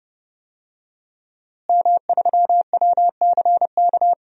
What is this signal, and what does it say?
Morse code, heard as a keyed tone, about 30 wpm, 700 Hz. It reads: M3WCK
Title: DXCC-100 Call Signs - 5 Characters - Repeated 2x - 30wpm